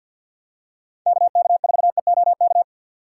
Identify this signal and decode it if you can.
Morse code, heard as a keyed tone, about 52 wpm, 700 Hz. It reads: KK4EOK
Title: USA Call Signs - 6 Characters - 3x Character Spacing - 50wpm